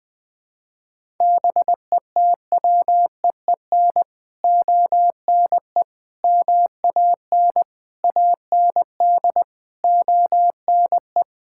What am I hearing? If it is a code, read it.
Morse code, heard as a keyed tone, about 20 wpm, 700 Hz. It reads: BETWEEN ONE MAN AND ONE
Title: Sets of 6 Words - 20wpm